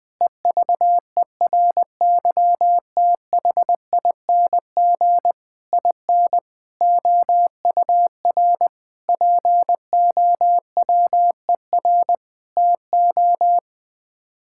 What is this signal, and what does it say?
Morse code, heard as a keyed tone, about 21 wpm, 700 Hz. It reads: EVERYTHING IN OUR POWER TO
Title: Sets of 8 Words - 20wpm